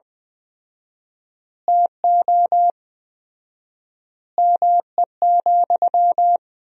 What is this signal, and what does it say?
Morse code, heard as a keyed tone, about 20 wpm, 700 Hz. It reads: TO ME,
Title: 4x Word-Spacing - Sentences from Top 500 Words - 20wpm